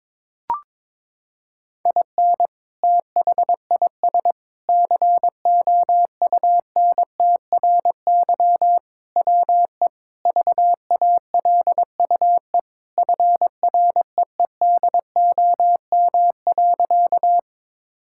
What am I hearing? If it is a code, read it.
Morse code, heard as a keyed tone, about 22 wpm, 700 Hz. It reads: IN THIS COUNTRY WE VALUE FREEDOM.
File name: practice / 22wpm / Sentences with 3-Word Phrases - 22wpm